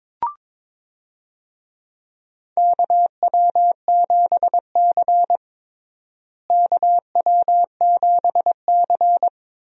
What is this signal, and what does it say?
Morse code, heard as a keyed tone, about 22 wpm, 700 Hz. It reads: KW7C KW7C
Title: USA Call Signs - 4 Characters - Repeated 2x - 22wpm